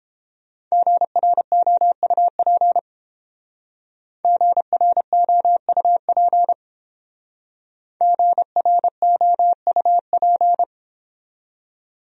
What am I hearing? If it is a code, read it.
Morse code, heard as a keyed tone, about 30 wpm, 700 Hz. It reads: GROUP GROUP GROUP